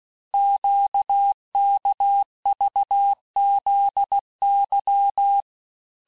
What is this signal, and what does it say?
Morse code, heard as a keyed tone, about 16 wpm, 800 Hz. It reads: QKVZY
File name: cours-du-lundi files / cours-1592--teste-chiffre-vitesse-15mpm